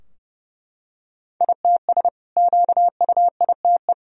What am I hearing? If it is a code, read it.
Morse code, heard as a keyed tone, about 30 wpm, 700 Hz. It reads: ITS QUITE
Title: Sentences from Top 1000 Words - Review Part 4 - 30wpm